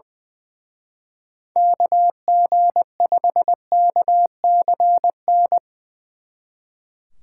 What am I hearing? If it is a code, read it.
Morse code, heard as a keyed tone, about 24 wpm, 700 Hz. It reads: KG5KCN
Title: Speed Racing - USA 2x3 Call Signs 20wpm